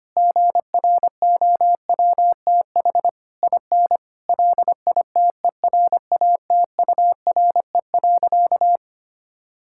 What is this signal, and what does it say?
Morse code, heard as a keyed tone, about 25 wpm, 700 Hz. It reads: GROWTH IN LITERATURE.
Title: Sentences from Top 1000 Words - Review Part 5 - 25wpm